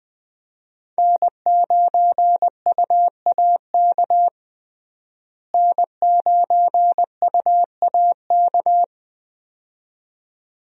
Morse code, heard as a keyed tone, about 20 wpm, 700 Hz. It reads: N9UAK N9UAK
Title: USA Call Signs - 5 Characters - Repeated 2x - 20wpm